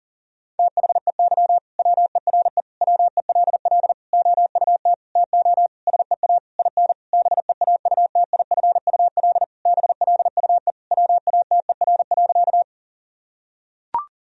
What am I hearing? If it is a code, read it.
Morse code, heard as a keyed tone, about 40 wpm, 700 Hz. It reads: THEY WERE WELL OUT TO SEA IN BEAUTIFUL BLUE WATER.
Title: Sentences from Top 1000 Words - Review Part 3 - 40wpm